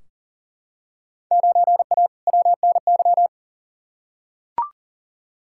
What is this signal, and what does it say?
Morse code, heard as a keyed tone, about 40 wpm, 700 Hz. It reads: 9A WNY
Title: Field Day Exchanges Encore - 40wpm